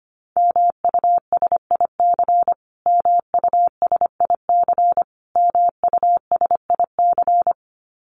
Morse code, heard as a keyed tone, about 25 wpm, 700 Hz. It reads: MUSIC MUSIC MUSIC
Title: Top 300 Words Repeated 25wpm